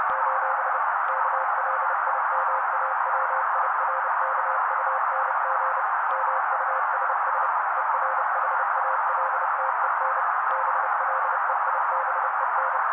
Morse code, heard as a keyed tone, about 30 wpm, 530 Hz. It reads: : KLIMAWIRKUNG MUSS ERHALTEN BLEIBEN
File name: daily / 29wpm noise